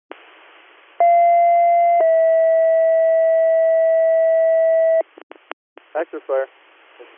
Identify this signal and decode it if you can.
Two-tone page, heard as a one-second tone, then a three-second tone, about 690 then 650 Hz, with speech. It is heard from 1.0 to 5.0 s.